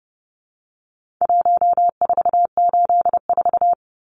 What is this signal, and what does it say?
Morse code, heard as a keyed tone, about 30 wpm, 700 Hz. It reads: 1484
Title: Numbers - Contest Style - 4-Digit 30wpm